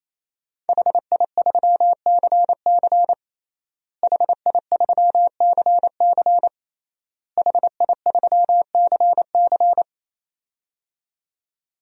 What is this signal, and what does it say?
Morse code, heard as a keyed tone, about 28 wpm, 700 Hz. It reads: HI3CC HI3CC HI3CC
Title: DXCC-100 Call Signs - 5 Characters - Repeated 3x - 28wpm